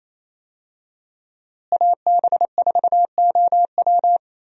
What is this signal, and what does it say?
Morse code, heard as a keyed tone, about 27 wpm, 700 Hz. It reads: AB4OW
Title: USA Call Signs - 5 Characters - Repeated 3x - 28wpm